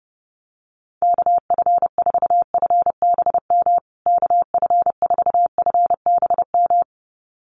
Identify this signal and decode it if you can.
Morse code, heard as a keyed tone, about 30 wpm, 700 Hz. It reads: KF4FBM KF4FBM
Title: USA Call Signs Repeated 2x 30wpm